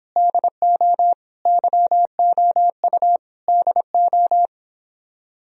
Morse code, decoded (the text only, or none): DO YOU DO